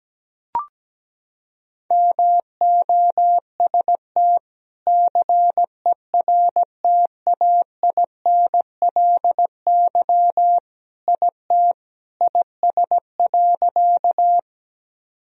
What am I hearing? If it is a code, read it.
Morse code, heard as a keyed tone, about 17 wpm, 700 Hz. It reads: MOST CERTAINLY IT IS.
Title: Sentences from Top 1000 Words - Review Part 1 - 17wpm